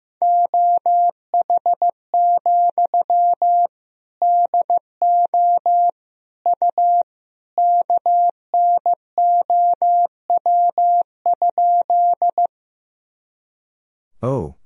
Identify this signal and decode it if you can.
Morse code, heard as a keyed tone, about 15 wpm, 700 Hz. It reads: OH, DO U KNOW?
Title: Sentences from Top 500 Words and Abbreviations 15wpm